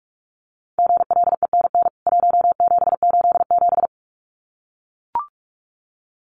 Morse code, heard as a keyed tone, about 44 wpm, 700 Hz. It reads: GLENN 1787
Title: CWT Exchange 45wpm